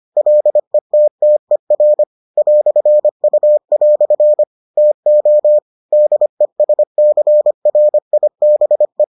Morse code, heard as a keyed tone, about 25 wpm, 600 Hz. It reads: LETTER "U" TO DESCRIBE